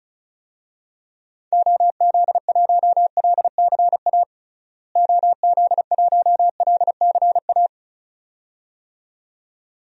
Morse code, heard as a keyed tone, about 36 wpm, 700 Hz. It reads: OZ1LCA OZ1LCA
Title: DXCC-100 Call Signs - 6 Characters - Repeated 2x - 35wpm